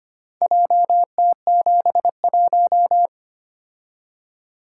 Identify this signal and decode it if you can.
Morse code, heard as a keyed tone, about 25 wpm, 700 Hz. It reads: JT71
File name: practice / 25wpm / Four Letter-Number Combinations - 25wpm